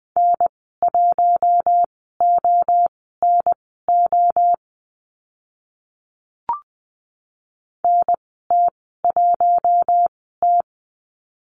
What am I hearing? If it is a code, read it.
Morse code, heard as a keyed tone, about 20 wpm, spaced as for 14 wpm, 700 Hz. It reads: N1ONO NT1T
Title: 048 Lesson 9g - Number 4 - Call Signs 20wpm